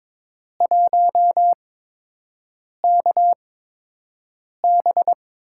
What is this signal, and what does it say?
Morse code, heard as a keyed tone, about 22 wpm, spaced as for 6 wpm, 700 Hz. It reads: 1KB
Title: USA Call Signs - 8x Character Spacing - 22wpm